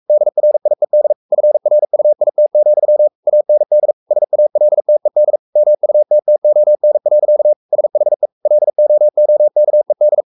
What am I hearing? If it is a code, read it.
Morse code, heard as a keyed tone, about 44 wpm, 600 Hz. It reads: DRIED FRUIT, AND SALTED MUTTON. SHE LOOKED
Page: Romanen Never af Ken Follet på engelsk.